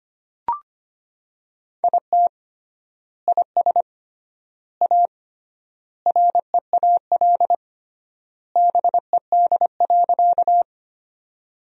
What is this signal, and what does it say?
Morse code, heard as a keyed tone, about 25 wpm, 700 Hz. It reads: IT IS A REAL BED.
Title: 3x Word-Spacing - Sentences from Top 500 Words - 25wpm